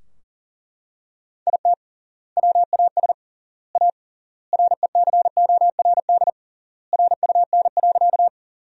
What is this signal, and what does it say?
Morse code, heard as a keyed tone, about 40 wpm, 700 Hz. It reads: IT WAS A RECORD RUN.
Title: 3x Word-Spacing - Sentences from Top 500 Words - 40wpm